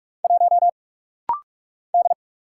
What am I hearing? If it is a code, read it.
Morse code, heard as a keyed tone, about 44 wpm, 700 Hz. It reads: 1 D
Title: Single Letter-Number Alt - Rapid-Fire 45wpm